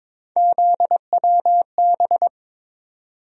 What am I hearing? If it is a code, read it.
Morse code, heard as a keyed tone, about 22 wpm, 700 Hz. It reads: ZWB
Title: USA Call Sign Postfixes - 3 Letters 22wpm